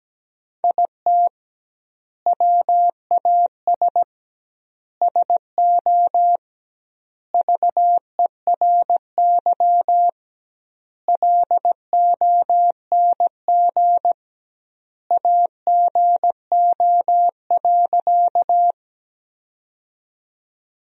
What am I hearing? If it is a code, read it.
Morse code, heard as a keyed tone, about 17 wpm, 700 Hz. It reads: IT WAS SO VERY LONG AGO.
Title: 2x Word-Spacing - Sentences from Top 500 Words - 17wpm